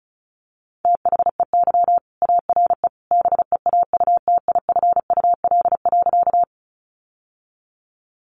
Morse code, heard as a keyed tone, about 34 wpm, 700 Hz. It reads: THEY ARE BEAUTIFUL.
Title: Sentences from Top 900 Words 35wpm